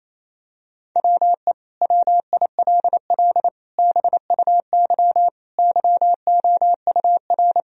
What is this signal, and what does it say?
Morse code, heard as a keyed tone, about 27 wpm, 700 Hz. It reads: WE WILL BUY YOUR
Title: Sentences from Top 1000 Words - Review Part 4 - 28wpm